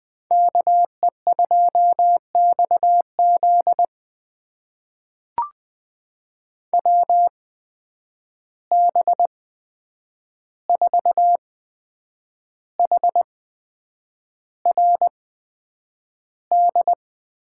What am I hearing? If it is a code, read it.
Morse code, heard as a keyed tone, about 20 wpm, spaced as for 8 wpm, 700 Hz. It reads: KE2XZ WB4HRD
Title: USA Call Signs - 8x Character Spacing - 20wpm